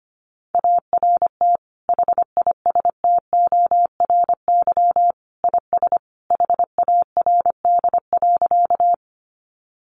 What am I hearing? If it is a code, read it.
Morse code, heard as a keyed tone, about 25 wpm, 700 Hz. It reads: ART HISTORY IS HARD.
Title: Sentences from Top 600 Words 25wpm